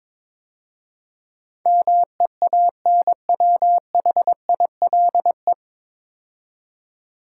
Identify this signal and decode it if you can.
Morse code, heard as a keyed tone, about 22 wpm, 690 Hz. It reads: MEANWHILE
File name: 9 Letter Words - First 3 Letters Given - 22wpm